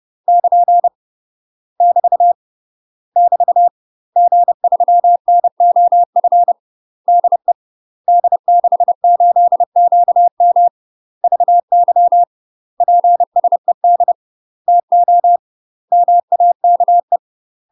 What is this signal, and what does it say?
Morse code, heard as a keyed tone, about 30 wpm, 700 Hz. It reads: <KN> <BT> <BT> G3NOF DE D68QM VY PSED TO MAKE
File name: UK MASTER QSOs TRAINING / 30WPM / UK MASTER QSOs 0024